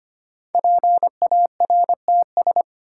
Morse code, heard as a keyed tone, about 25 wpm, 700 Hz. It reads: PARTS
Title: Sets of 3 Words - New Word Drill - 25wpm